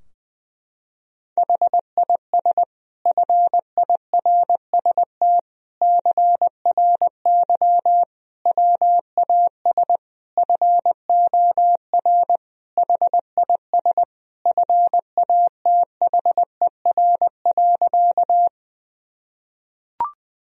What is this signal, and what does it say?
Morse code, heard as a keyed tone, about 21 wpm, 700 Hz. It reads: HIS FIRST CRY WAS FOR HIS FATHER.
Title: Sentences from Top 1000 Words - Review Part 2 - 20wpm